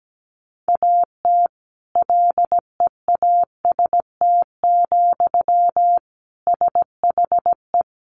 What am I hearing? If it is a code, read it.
Morse code, heard as a keyed tone, about 17 wpm, 700 Hz. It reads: AT LEAST, SHE
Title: Sentences from Top 1400 Words 17wpm